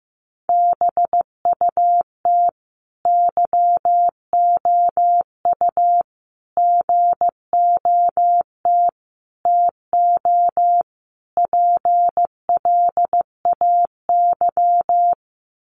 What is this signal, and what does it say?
Morse code, heard as a keyed tone, about 15 wpm, 700 Hz. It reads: BUT YOU GOT TO PLAY
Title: Sentences from Top 300 Words 15wpm